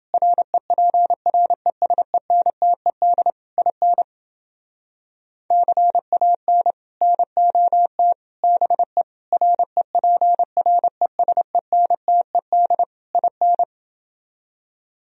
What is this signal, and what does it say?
Morse code, heard as a keyed tone, about 30 wpm, 700 Hz. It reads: REPRESENTED IN CAN NOT BE REPRESENTED IN